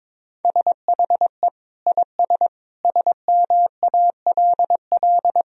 Morse code, heard as a keyed tone, about 22 wpm, 700 Hz. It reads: SHE IS SMALL
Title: Sentences from Top 1000 Words - Review Part 1 - 22wpm